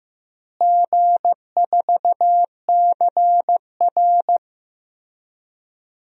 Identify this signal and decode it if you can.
Morse code, heard as a keyed tone, about 16 wpm, 700 Hz. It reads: G4CR
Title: DXCC-100 Call Signs - 4 Characters - Repeated 3x - 15wpm